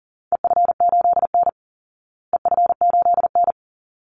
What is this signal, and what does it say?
Morse code, heard as a keyed tone, about 40 wpm, 700 Hz. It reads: EF8N EF8N
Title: Call Signs Repeated 3x - 40wpm